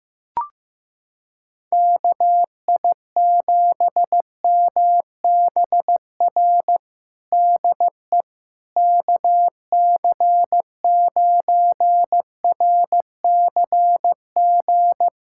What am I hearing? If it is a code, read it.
Morse code, heard as a keyed tone, about 15 wpm, 700 Hz. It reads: KI7MBR DE KC9RCG
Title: USA Callsign DE Callsign - 6 Characters - 15wpm